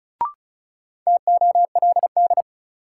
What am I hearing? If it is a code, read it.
Morse code, heard as a keyed tone, about 36 wpm, 700 Hz. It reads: TOLD
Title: Sentences from Top 400 Words - New Word Drill - 35wpm